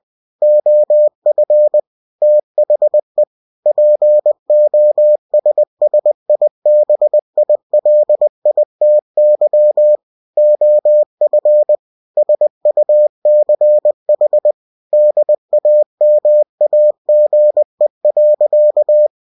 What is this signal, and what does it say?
Morse code, heard as a keyed tone, about 20 wpm, 600 Hz. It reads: OF THE POSSIBILITY OF SUCH DAMAGE.
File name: cw-books / winnie-the-pooh / winnie-20wpm-page-0119